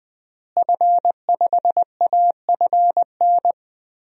Morse code, heard as a keyed tone, about 20 wpm, 700 Hz. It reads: F5AFN
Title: DXCC-100 Call Signs - 5 Characters - Repeated 3x - 20wpm